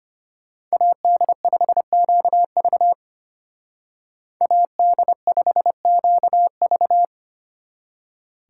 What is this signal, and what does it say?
Morse code, heard as a keyed tone, about 30 wpm, 700 Hz. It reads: AD5QV AD5QV
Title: Speed Racing - USA 2x2 Call Signs 25wpm